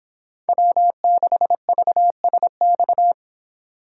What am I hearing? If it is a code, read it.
Morse code, heard as a keyed tone, about 26 wpm, 700 Hz. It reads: W6VSX